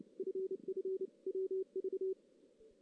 Morse code, heard as a keyed tone, about 29 wpm, 380 Hz. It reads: FFWV
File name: 2016 JAN CW NAQP / 2016 Jan naqp k5ot - 0-3